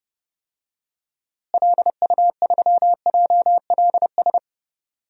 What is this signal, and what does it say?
Morse code, heard as a keyed tone, about 30 wpm, 700 Hz. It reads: LU3JLS
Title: DXCC-100 Call Signs - 6 Characters - Repeated 3x - 30wpm